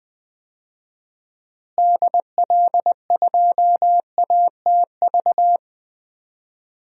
Morse code, heard as a keyed tone, about 21 wpm, 700 Hz. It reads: DL2ATV
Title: DXCC-100 Call Signs - 6 Characters - Repeated 3x - 20wpm